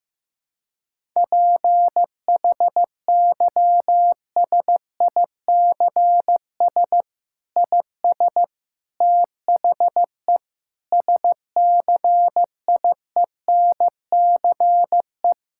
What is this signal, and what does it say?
Morse code, heard as a keyed tone, about 15 wpm, 700 Hz. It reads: PHYSICS IS THE SCIENCE
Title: Sentences from Top 1000 Words - Review Part 5 - 15wpm